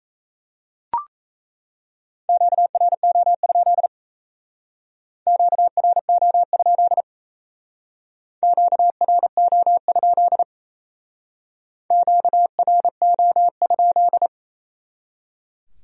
Morse code, heard as a keyed tone, about 36 wpm, 700 Hz. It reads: QRO? QRO? QRO? QRO?